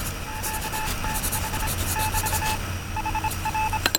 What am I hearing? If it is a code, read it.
Morse code, heard as a keyed tone, about 26 wpm, 870 Hz, with noise. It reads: QTH <BT> HR